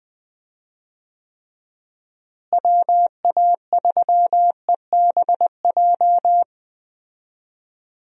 Morse code, heard as a keyed tone, about 20 wpm, 700 Hz. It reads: WA3EBJ
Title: USA Call Signs - 6 Characters - Repeated 3x - 20wpm